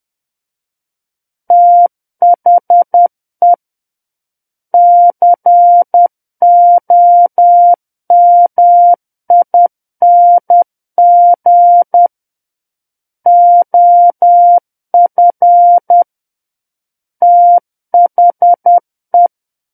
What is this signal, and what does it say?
Morse code, heard as a keyed tone, about 10 wpm, 700 Hz. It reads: THE COMING OF THE
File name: Book One / 01-Chapter 1 - 10 WPM